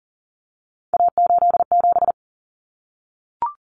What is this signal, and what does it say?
Morse code, heard as a keyed tone, about 40 wpm, 700 Hz. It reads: A87
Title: Common Cut Numbers - 3-Digit 40wpm